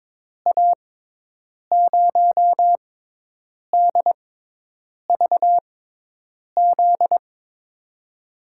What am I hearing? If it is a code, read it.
Morse code, heard as a keyed tone, about 22 wpm, spaced as for 8 wpm, 700 Hz. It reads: A0DVZ
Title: USA Call Signs - 6 Characters - 6x Character Spacing - 22wpm